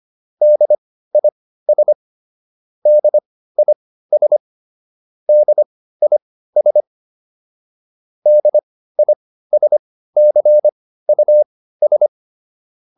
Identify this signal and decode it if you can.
Morse code, heard as a keyed tone, about 25 wpm, spaced as for 15 wpm, 600 Hz. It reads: DIS DIS DIS DISCUS